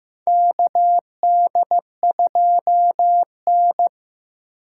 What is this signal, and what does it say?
Morse code, heard as a keyed tone, about 15 wpm, 700 Hz. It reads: KD2N
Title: USA Call Signs - 4 Characters - 4x Character Spacing - 15wpm